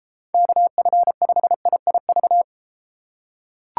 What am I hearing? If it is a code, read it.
Morse code, heard as a keyed tone, about 32 wpm, 700 Hz. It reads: KF5IIV